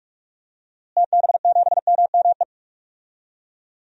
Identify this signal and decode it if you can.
Morse code, heard as a keyed tone, about 46 wpm, 700 Hz. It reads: TB7MME
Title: DXCC-100 Call Signs - 6 Characters - Repeated 3x - 45wpm